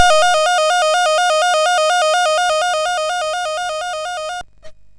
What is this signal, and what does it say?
There is no code.